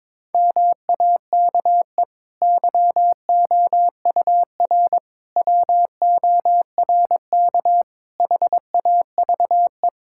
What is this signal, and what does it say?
Morse code, heard as a keyed tone, about 21 wpm, 700 Hz. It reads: MAKE YOUR WORK HAVE
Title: Sentences from Top 1000 Words - Review Part 6 - 22wpm